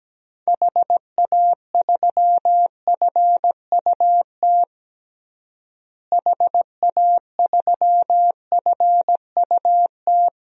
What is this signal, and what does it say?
Morse code, heard as a keyed tone, about 17 wpm, 700 Hz. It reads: HA3FUT HA3FUT
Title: DXCC-100 Call Signs - 6 Characters - Repeated 2x - 17wpm